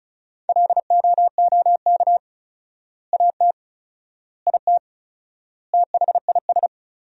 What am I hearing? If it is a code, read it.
Morse code, heard as a keyed tone, about 36 wpm, 700 Hz. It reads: LOOK AT IT THIS
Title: 4x Word-Spacing - Sentences from Top 100 Words - 35wpm